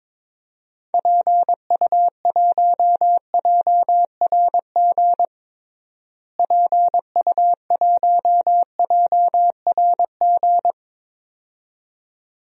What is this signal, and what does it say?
Morse code, heard as a keyed tone, about 22 wpm, 700 Hz. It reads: PU1JRG PU1JRG
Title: DXCC-100 Call Signs - 6 Characters - Repeated 2x - 22wpm